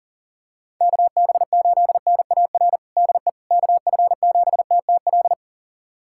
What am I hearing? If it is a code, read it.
Morse code, heard as a keyed tone, about 37 wpm, 700 Hz. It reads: KB8NAR DE KF7TTL
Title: USA Callsign DE Callsign - 6 Characters - 40wpm